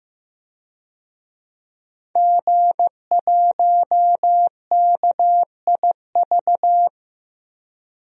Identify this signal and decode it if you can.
Morse code, heard as a keyed tone, about 15 wpm, 700 Hz. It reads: G1KIV
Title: DXCC-100 Call Signs - 5 Characters - Repeated 2x - 15wpm